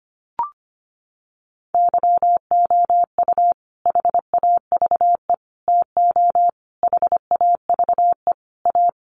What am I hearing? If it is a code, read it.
Morse code, heard as a keyed tone, about 25 wpm, 700 Hz. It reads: YOU HAVE TO HAVE A
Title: Sets of 5 Words Encore 25wpm